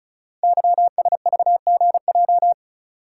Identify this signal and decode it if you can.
Morse code, heard as a keyed tone, about 36 wpm, 700 Hz. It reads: YSVGJ
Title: Code Groups - 5 Letters - 35wpm